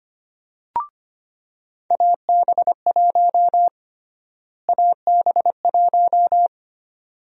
Morse code, heard as a keyed tone, about 25 wpm, 700 Hz. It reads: AB1 AB1
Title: USA Call Sign Prefixes - Repeated 2x - 25wpm